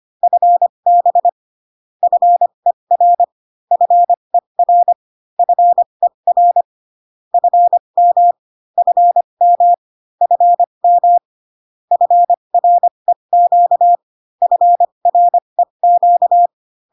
Morse code, heard as a keyed tone, about 25 wpm, spaced as for 22 wpm, 700 Hz. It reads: FB FER FER FER FM FM FM FREQ FREQ
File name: CW common abbreviations alphabetical repeated 3 times 25 18wpm